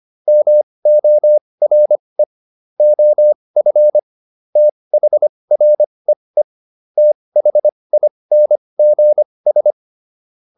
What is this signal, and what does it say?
Morse code, heard as a keyed tone, about 25 wpm, spaced as for 20 wpm, 600 Hz. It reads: MORE OF THREE THINGS
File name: SS306 20